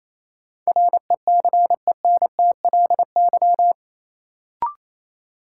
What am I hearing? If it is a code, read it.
Morse code, heard as a keyed tone, about 28 wpm, 700 Hz. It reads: RECENTLY
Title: Sentences from Top 1200 Words - New Word Drill Alt - 28wpm